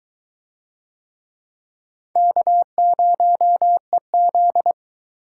Morse code, heard as a keyed tone, about 21 wpm, 700 Hz. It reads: K0EZ